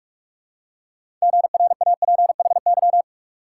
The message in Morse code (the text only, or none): GRAPHY